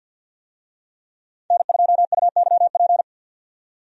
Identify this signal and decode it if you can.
Morse code, heard as a keyed tone, about 48 wpm, 700 Hz. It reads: N2UYP
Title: USA Call Signs - 5 Characters - 50wpm